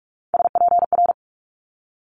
Morse code, heard as a keyed tone, about 46 wpm, 700 Hz. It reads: SPR
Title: Multi-letter Phonemes Quick Fire 45wpm